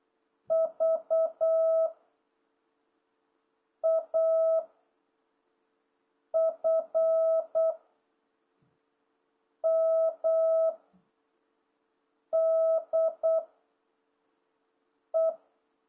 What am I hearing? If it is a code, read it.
Morse code, heard as a keyed tone, about 8 wpm, 660 Hz. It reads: V A F M D E